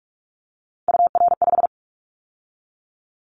Morse code, heard as a keyed tone, about 46 wpm, 700 Hz. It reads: UR5
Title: Callsign Prefixes for Top 25 DXCC - 45wpm